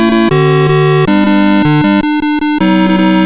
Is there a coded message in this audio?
No.